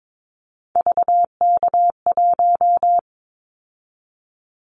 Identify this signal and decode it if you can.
Morse code, heard as a keyed tone, about 22 wpm, 700 Hz. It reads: VK1
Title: Callsign Prefixes for Top 100 DXCC - 22wpm